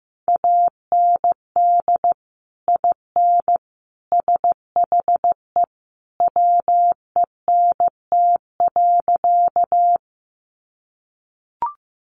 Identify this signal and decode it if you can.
Morse code, heard as a keyed tone, about 15 wpm, 700 Hz. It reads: AND IN SHE WENT.
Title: Sentences from Top 200 Words 15wpm